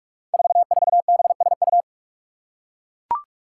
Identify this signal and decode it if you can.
Morse code, heard as a keyed tone, about 41 wpm, 700 Hz. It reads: 44BSU
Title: Five Challenging Letter-Number Combinations 45wpm